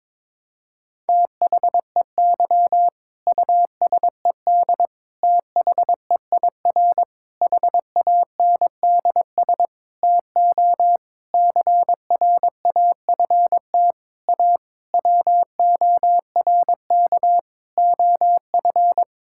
Morse code, heard as a keyed tone, about 22 wpm, 700 Hz. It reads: THEY USED THEIR HANDS TO CRAFT A WORK OF
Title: Sentences from Top 1000 Words - Review Part 6 - 22wpm